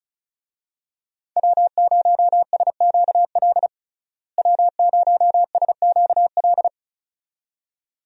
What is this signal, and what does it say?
Morse code, heard as a keyed tone, about 34 wpm, 700 Hz. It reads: W0SQL W0SQL
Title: USA Call Signs - 5 Characters - Repeated 2x - 35wpm